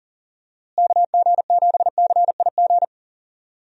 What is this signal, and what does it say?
Morse code, heard as a keyed tone, about 40 wpm, 700 Hz. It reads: KG7CIG
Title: USA Call Signs - 6 Characters - Repeated 2x - 40wpm